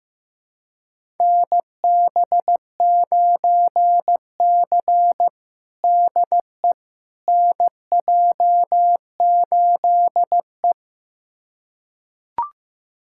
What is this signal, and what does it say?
Morse code, heard as a keyed tone, about 15 wpm, 700 Hz. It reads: NB9C DE NJ8E
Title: USA Callsign DE Callsign - 4 Characters - 15wpm